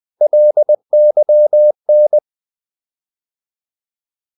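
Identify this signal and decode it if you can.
Morse code, heard as a keyed tone, about 20 wpm, 600 Hz. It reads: LYN
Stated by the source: Morse code practice of short Danish words at 20 wpm, from CW med Gnister.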